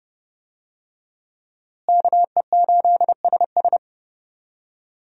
Morse code, heard as a keyed tone, about 30 wpm, 700 Hz. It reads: KE8SS
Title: USA Call Signs - 5 Characters - Repeated 2x - 30wpm